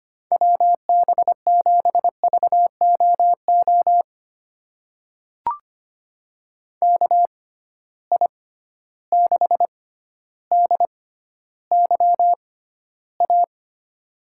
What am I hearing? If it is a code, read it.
Morse code, heard as a keyed tone, about 25 wpm, spaced as for 13 wpm, 700 Hz. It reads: WB7VOO KI6DYA
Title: USA Call Signs - 6 Characters - 6x Character Spacing - 25wpm